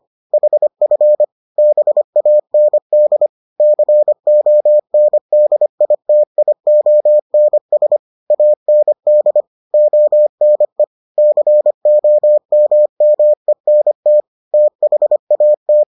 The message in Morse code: HF BAND CONDITIONS AND ONE COMMENT THAT

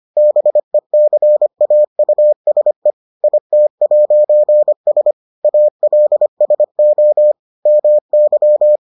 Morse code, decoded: BECAUSE IT'S ALSO MY

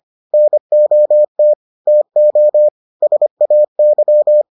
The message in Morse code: NOT TO SAY

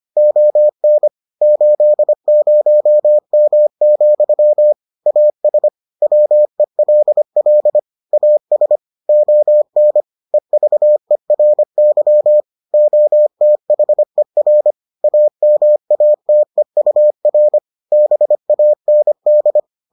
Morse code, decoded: ON 80M, AS WELL AS ON EVERY OTHER AMATEUR BAND

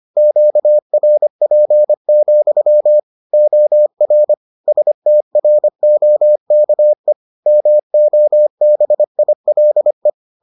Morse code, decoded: QRP, OR STROKE MOBILE

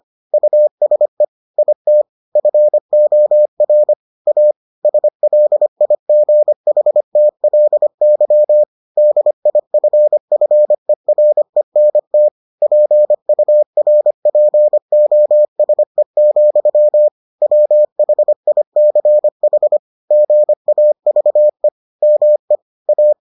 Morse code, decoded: USE IT FOR A SLIGHTLY DIFFERENT PURPOSE, WHICH GAVE ME A

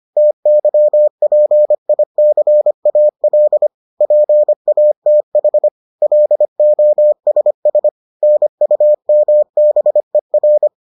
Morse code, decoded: TYPICAL PATH LOSS NUMBER

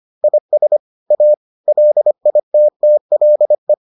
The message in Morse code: IS A LITTLE

